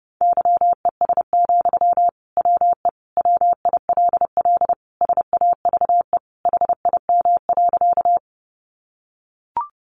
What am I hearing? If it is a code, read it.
Morse code, heard as a keyed tone, about 30 wpm, 700 Hz. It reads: YES, WE WILL SAVE HIM.
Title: Sentences from Top 1100 Words 30wpm